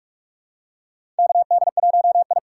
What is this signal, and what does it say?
Morse code, heard as a keyed tone, about 44 wpm, 700 Hz. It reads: KD1I
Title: USA Call Signs - 4 Characters - 6x Character Spacing - 45wpm